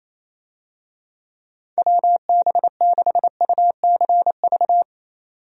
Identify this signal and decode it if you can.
Morse code, heard as a keyed tone, about 28 wpm, 700 Hz. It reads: WB6UCV
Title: USA Call Signs - 6 Characters - 28wpm